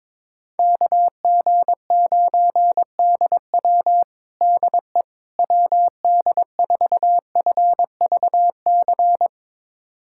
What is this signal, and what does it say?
Morse code, heard as a keyed tone, about 22 wpm, 700 Hz. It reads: KG9DW DE WD4FVC
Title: USA Callsign DE Callsign 22wpm